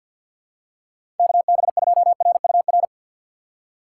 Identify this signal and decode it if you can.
Morse code, heard as a keyed tone, about 53 wpm, 700 Hz. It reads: KB2RUR
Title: USA Call Signs - 3x Character Spacing - 50wpm